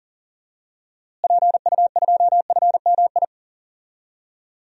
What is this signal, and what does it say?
Morse code, heard as a keyed tone, about 40 wpm, 700 Hz. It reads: PU2FMI
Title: DXCC-100 Call Signs - 6 Characters - Repeated 3x - 40wpm